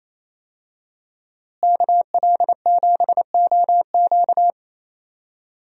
Morse code, heard as a keyed tone, about 28 wpm, 700 Hz. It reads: KL7OQ
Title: DXCC-100 Call Signs - 5 Characters - Repeated 3x - 28wpm